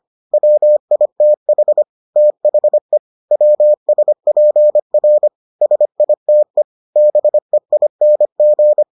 Morse code, heard as a keyed tone, about 25 wpm, 600 Hz. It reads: WITH THE WSPR SITE BEING